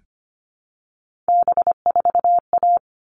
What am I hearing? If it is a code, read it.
Morse code, heard as a keyed tone, about 25 wpm, 700 Hz. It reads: B4A
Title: All Cut Numbers - 3-Digit 25wpm